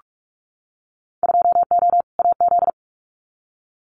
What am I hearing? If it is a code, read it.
Morse code, heard as a keyed tone, about 44 wpm, 700 Hz. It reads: 2O AZ
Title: Winter Field Day Exchanges 45wpm